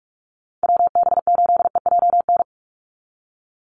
Morse code, heard as a keyed tone, about 44 wpm, 700 Hz. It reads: WB8EJN
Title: Call Signs - 45wpm